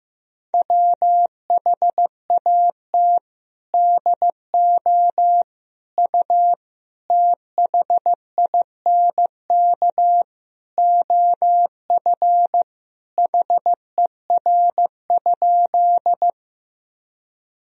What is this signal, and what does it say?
Morse code, heard as a keyed tone, about 16 wpm, 700 Hz. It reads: WHAT DO U THINK OF HER?
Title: Sentences from Top 200 Words and Abbreviations 15wpm